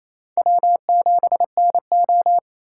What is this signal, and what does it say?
Morse code, heard as a keyed tone, about 33 wpm, 700 Hz. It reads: W7NO
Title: Speed Racing - USA 1x2 Call Signs 28wpm